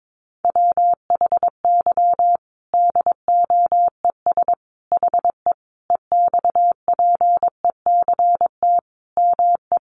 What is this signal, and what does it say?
Morse code, heard as a keyed tone, about 22 wpm, 700 Hz. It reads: WHY DOES HE EXPECT ME
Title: Sentences from Top 1100 Words 22wpm